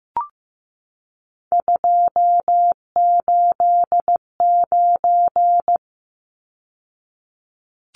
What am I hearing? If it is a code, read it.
Morse code, heard as a keyed tone, about 15 wpm, 700 Hz. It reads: 289
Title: Numbers - 3-Digit 15wpm